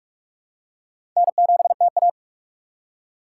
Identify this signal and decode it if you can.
Morse code, heard as a keyed tone, about 46 wpm, 700 Hz. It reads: N7TA
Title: USA Call Signs - 4 Characters - 3x Character Spacing - 45wpm